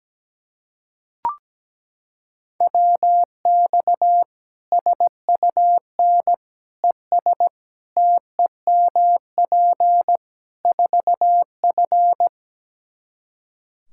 Morse code, heard as a keyed tone, about 17 wpm, 700 Hz. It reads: WX SUN ES TEMP 4F
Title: Practice QSO - Exchange 2 - Info - 17wpm